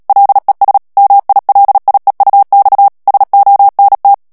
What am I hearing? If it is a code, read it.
Morse code, heard as a keyed tone, about 37 wpm, 800 Hz. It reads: LES MILIEUX SONT
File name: CW / FICHIERS-MP3 / FRANCAIS TEXTE CLAIR 2 VITESSE 35 MOTS MINUTES